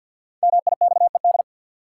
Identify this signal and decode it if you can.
Morse code, heard as a keyed tone, about 53 wpm, 700 Hz. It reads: MIXED
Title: Sentences from Top 1500 Words - New Word Drill - 50wpm